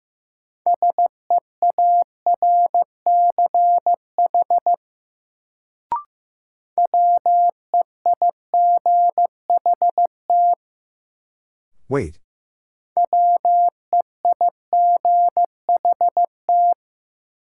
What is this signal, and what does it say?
Morse code, heard as a keyed tone, about 15 wpm, 700 Hz. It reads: SEARCH WEIGHT WEIGHT
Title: Sentences from Top 800 Words - New Word Drill - 15wpm